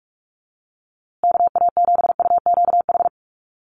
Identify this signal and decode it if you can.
Morse code, heard as a keyed tone, about 44 wpm, 700 Hz. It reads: KA7UQH
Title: USA Call Signs Repeated 2x 45wpm